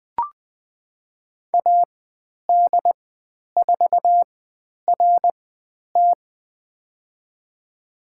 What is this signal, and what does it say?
Morse code, heard as a keyed tone, about 20 wpm, spaced as for 10 wpm, 700 Hz. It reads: AD4RT
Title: 095 Lesson 16g - Letter c - Call Signs 20wpm